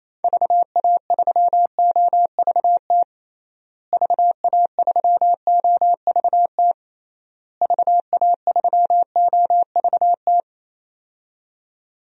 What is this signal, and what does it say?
Morse code, heard as a keyed tone, about 28 wpm, 700 Hz. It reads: VA3OVT VA3OVT VA3OVT
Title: DXCC-100 Call Signs - 6 Characters - Repeated 3x - 28wpm